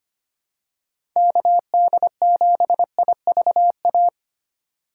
Morse code, heard as a keyed tone, about 25 wpm, 700 Hz. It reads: KD7IVA
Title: USA Call Signs - 2x Character Spacing - 25wpm